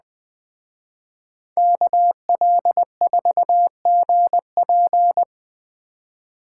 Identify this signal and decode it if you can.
Morse code, heard as a keyed tone, about 20 wpm, 700 Hz. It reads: KL4GP
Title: Speed Racing - USA 2x2 Call Signs 17wpm